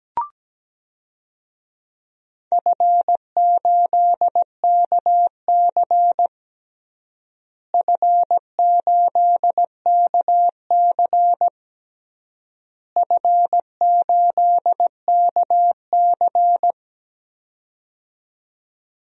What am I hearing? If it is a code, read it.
Morse code, heard as a keyed tone, about 17 wpm, 700 Hz. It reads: F8KC F8KC F8KC
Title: DXCC-100 Call Signs - 4 Characters - Repeated 3x - 17wpm